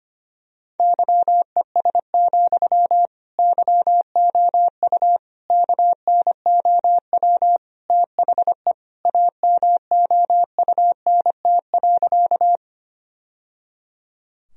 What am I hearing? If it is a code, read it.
Morse code, heard as a keyed tone, about 25 wpm, 700 Hz. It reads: YES, YOU KNOW THE AMOUNT.
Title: Sentences from Top 1000 Words - Review Part 4 - 25wpm